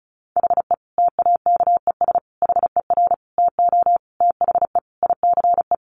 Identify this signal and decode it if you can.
Morse code, heard as a keyed tone, about 34 wpm, 700 Hz. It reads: HE TAKES HER TO THE ICE
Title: Easy Phrases 35wpm